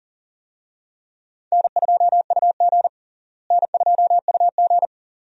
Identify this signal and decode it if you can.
Morse code, heard as a keyed tone, about 40 wpm, 700 Hz. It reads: N2UG N2UG
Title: USA Call Signs - 4 Characters - Repeated 2x - 40wpm